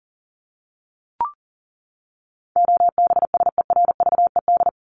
Morse code, heard as a keyed tone, about 40 wpm, 700 Hz. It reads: OBSERVED
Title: ED Suffix - Top 100 Words with Base Words 40wpm